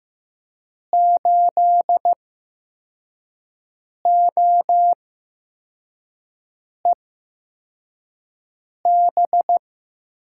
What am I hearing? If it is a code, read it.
Morse code, heard as a keyed tone, about 15 wpm, spaced as for 4 wpm, 700 Hz. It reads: 8OEB
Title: USA Call Signs - 8x Character Spacing - 15wpm